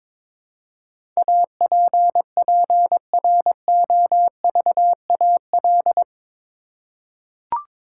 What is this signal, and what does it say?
Morse code, heard as a keyed tone, about 22 wpm, 700 Hz. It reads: APPROVAL
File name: practice / 22wpm / Eight Letter Words - 22wpm